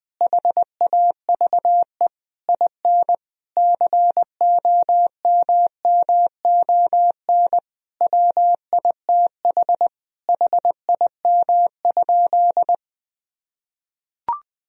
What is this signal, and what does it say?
Morse code, heard as a keyed tone, about 20 wpm, 700 Hz. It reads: HAVE IN COMMON WITH HIM?
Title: Sentences from Top 1000 Words - Review Part 3 - 20wpm